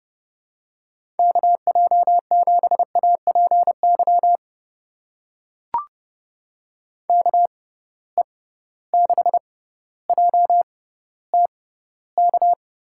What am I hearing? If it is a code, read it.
Morse code, heard as a keyed tone, about 32 wpm, spaced as for 16 wpm, 700 Hz. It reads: KJ7APY KE6JTK
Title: USA Call Signs - 6 Characters - 6x Character Spacing - 30wpm